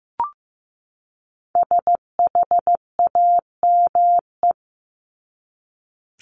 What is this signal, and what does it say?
Morse code, heard as a keyed tone, about 15 wpm, 700 Hz. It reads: SHAME
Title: Top 1751-2000 Words 15wpm